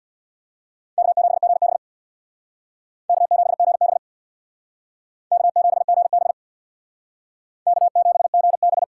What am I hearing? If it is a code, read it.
Morse code, heard as a keyed tone, about 61 wpm, 700 Hz. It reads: K7GB K7GB K7GB K7GB
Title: Speed Racing - USA 1x2 Call Signs 50wpm